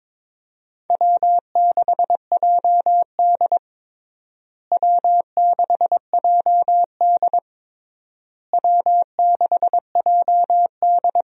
Morse code, heard as a keyed tone, about 21 wpm, 700 Hz. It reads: W6JD W6JD W6JD
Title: USA Call Signs - 4 Characters - Repeated 3x - 22wpm